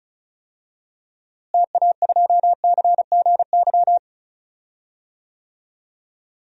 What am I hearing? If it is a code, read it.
Morse code, heard as a keyed tone, about 36 wpm, 700 Hz. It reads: TA2CGY
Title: DXCC-100 Call Signs - 6 Characters - Repeated 2x - 35wpm